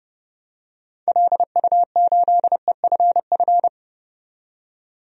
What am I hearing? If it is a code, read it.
Morse code, heard as a keyed tone, about 30 wpm, 700 Hz. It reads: LU8EFF
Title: DXCC-100 Call Signs - 6 Characters - 30wpm